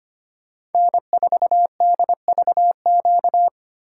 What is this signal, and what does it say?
Morse code, heard as a keyed tone, about 33 wpm, 700 Hz. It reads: N4DVQ